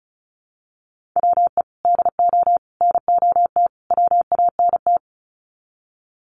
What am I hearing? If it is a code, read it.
Morse code, heard as a keyed tone, about 34 wpm, 700 Hz. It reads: WE DO NOT WANT
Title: Sets of 4 Words 35wpm